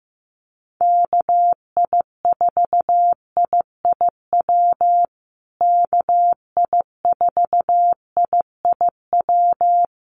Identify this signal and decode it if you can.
Morse code, heard as a keyed tone, about 14 wpm, 700 Hz. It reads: KI4IIW KI4IIW
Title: USA Call Signs Repeated 3x 15wpm